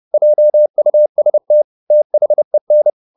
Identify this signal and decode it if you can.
Morse code, heard as a keyed tone, about 30 wpm, 600 Hz. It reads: JUST THEN
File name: the old man and the sea / the old man and the sea 1 30wpm